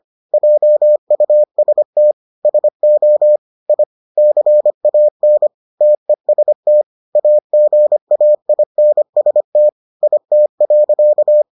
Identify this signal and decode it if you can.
Morse code, heard as a keyed tone, about 25 wpm, 600 Hz. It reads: JUST SO I CAN TEST AGAINST IT.